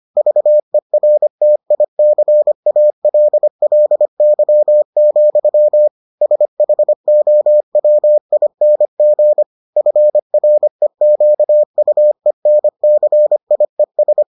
Morse code, heard as a keyed tone, about 25 wpm, 600 Hz. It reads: VERTICALLY, SHOWING FREQUENCIES